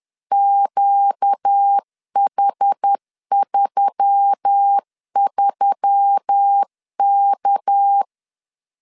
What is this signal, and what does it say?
Morse code, heard as a keyed tone, about 10 wpm, 800 Hz. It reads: QH33K
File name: CW / DET / 1 6 1500C S C 10wpm 4252